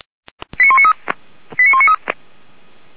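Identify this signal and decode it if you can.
Five-tone selective call (ZVEI), heard as quick runs of five tones, tones of about 70 ms. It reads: 89193 89193
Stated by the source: ZVEI five-tone selective call received by SDR.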